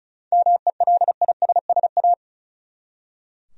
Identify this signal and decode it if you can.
Morse code, heard as a keyed tone, about 36 wpm, 700 Hz. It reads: MELISSA
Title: USA Common Names - 35wpm